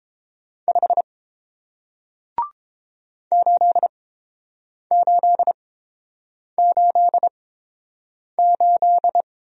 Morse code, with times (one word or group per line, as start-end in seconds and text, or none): 0.7-1.0 5
3.3-9.2 8888